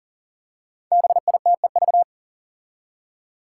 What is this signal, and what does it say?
Morse code, heard as a keyed tone, about 40 wpm, 700 Hz. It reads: BITEV
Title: Five Challenging Letter-Number Combinations 40wpm